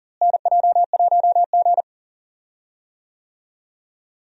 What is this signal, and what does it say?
Morse code, heard as a keyed tone, about 40 wpm, 700 Hz. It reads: NJ1G
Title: USA Call Signs - 4 Characters - Repeated 2x - 40wpm